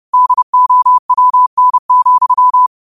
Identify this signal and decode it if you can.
Morse code, heard as a keyed tone, about 30 wpm, 1000 Hz. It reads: DOWN,
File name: qotdfiles / 30wpm